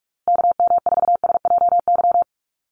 Morse code, heard as a keyed tone, about 44 wpm, 700 Hz. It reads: KM4SJY
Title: USA Call Signs Repeated 2x 45wpm